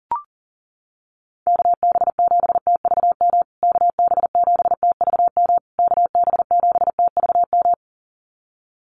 Morse code, heard as a keyed tone, about 40 wpm, 700 Hz. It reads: KB7TVM KB7TVM KB7TVM
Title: USA Call Signs Repeated 3x 40wpm